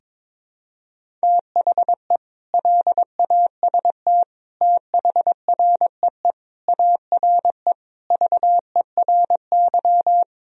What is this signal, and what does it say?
Morse code, heard as a keyed tone, about 21 wpm, 700 Hz. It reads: THE LAST THREE ARE VERY
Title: Sentences from Top 1000 Words - Review Part 1 - 22wpm